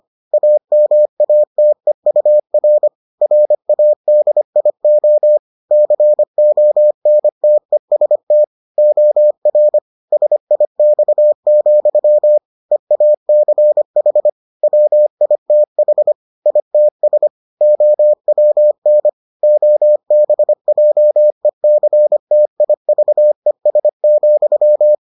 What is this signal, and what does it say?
Morse code, heard as a keyed tone, about 25 wpm, 600 Hz. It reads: AMATEUR RADIO CONTEST OR SIX, EACH WITH ITS OWN OBJECTIVES,